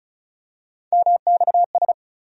Morse code, heard as a keyed tone, about 36 wpm, 700 Hz. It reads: MXS
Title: USA Call Sign Postfixes - 3 Letters 35wpm